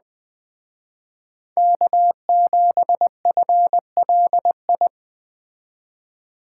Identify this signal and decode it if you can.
Morse code, heard as a keyed tone, about 20 wpm, 700 Hz. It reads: K7FLI
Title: Speed Racing - USA 1x3 Call Signs 17wpm